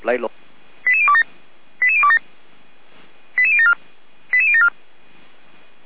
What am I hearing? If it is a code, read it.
Five-tone selective call (ZVEI), heard as quick runs of five tones, tones of about 70 ms, with speech. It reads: 88027 88027 88074 88074